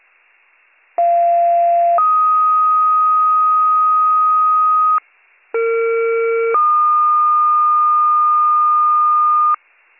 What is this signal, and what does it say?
Two-tone page, heard as a one-second tone, then a three-second tone, about 690 then 1250 Hz, about 460 then 1220 Hz. It is heard from 1.0 to 9.6 s.